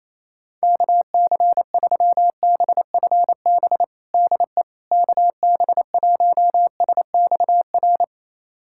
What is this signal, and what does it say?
Morse code, heard as a keyed tone, about 28 wpm, 700 Hz. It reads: KC3BFB DE KB1SXR
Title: USA Callsign DE Callsign - 6 Characters - 28wpm